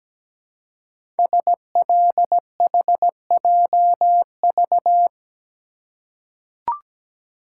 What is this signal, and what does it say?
Morse code, heard as a keyed tone, about 17 wpm, 700 Hz. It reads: SLHJV
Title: Code Groups - 5 Letters - 17wpm